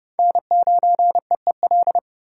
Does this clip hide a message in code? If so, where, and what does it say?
Yes: Morse code from 0.2 to 2.0 s, heard as a keyed tone, about 27 wpm, 700 Hz. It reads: N9EEL